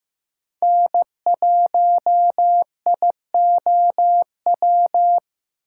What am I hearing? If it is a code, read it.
Morse code, heard as a keyed tone, about 15 wpm, 700 Hz. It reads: N1IOW
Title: USA Call Signs - 6x Character Spacing - 15wpm